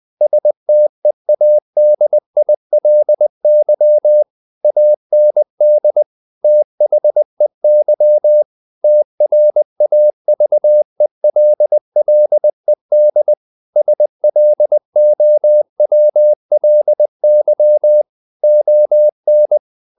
Morse code, heard as a keyed tone, about 20 wpm, 600 Hz. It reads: STEADILY AND THEY TRAVELLED SLOWLY ON
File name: the old man and the sea / the old man and the sea part 1 20wpm